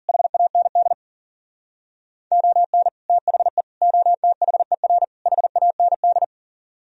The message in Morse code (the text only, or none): HAND ON THE OTHER HAND